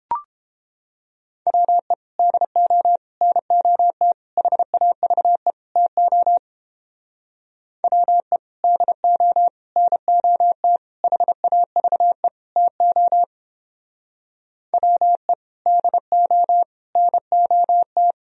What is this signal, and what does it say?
Morse code, heard as a keyed tone, about 29 wpm, 700 Hz. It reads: WE DO NOT HAVE TO WE DO NOT HAVE TO WE DO NOT